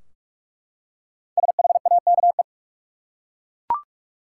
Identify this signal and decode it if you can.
Morse code, heard as a keyed tone, about 46 wpm, 700 Hz. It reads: SHAKE
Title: Five Letter Words - 45wpm